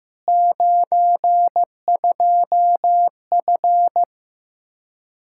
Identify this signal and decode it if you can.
Morse code, heard as a keyed tone, about 15 wpm, 700 Hz. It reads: 92F
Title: Practice QSO - Exchange 2 - Info - 15wpm